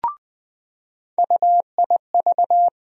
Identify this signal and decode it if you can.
Morse code, heard as a keyed tone, about 20 wpm, 700 Hz. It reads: UIV
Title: USA Call Sign Postfixes - 3 Letters 20wpm